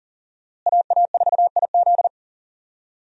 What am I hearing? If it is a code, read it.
Morse code, heard as a keyed tone, about 40 wpm, 700 Hz. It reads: AA4IZ
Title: USA Call Signs - 5 Characters - Repeated 3x - 40wpm